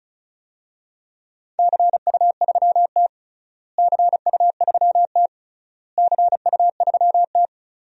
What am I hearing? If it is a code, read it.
Morse code, heard as a keyed tone, about 36 wpm, 700 Hz. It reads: CU3T CU3T CU3T
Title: DXCC-100 Call Signs - 4 Characters - Repeated 3x - 35wpm